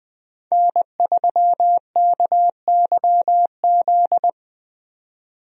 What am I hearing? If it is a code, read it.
Morse code, heard as a keyed tone, about 20 wpm, 700 Hz. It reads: N3KYZ
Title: USA Call Signs - 5 Characters - 20wpm